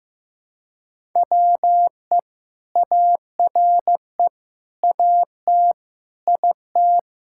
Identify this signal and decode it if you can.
Morse code, heard as a keyed tone, about 15 wpm, 700 Hz. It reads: WE ARE AT IT
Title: Sentences from Top 1000 Words - Review Part 5 - 15wpm